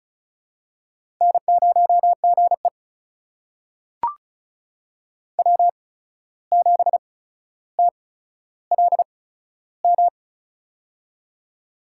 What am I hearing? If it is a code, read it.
Morse code, heard as a keyed tone, about 36 wpm, spaced as for 14 wpm, 700 Hz. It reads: N0GE W7TLM
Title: USA Call Signs - 8x Character Spacing - 35wpm